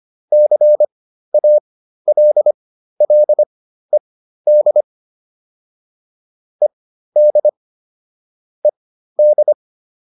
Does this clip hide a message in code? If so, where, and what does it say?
Yes: Morse code from 0.3 to 9.5 s, heard as a keyed tone, about 25 wpm, spaced as for 13 wpm, 600 Hz. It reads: CALLED ED ED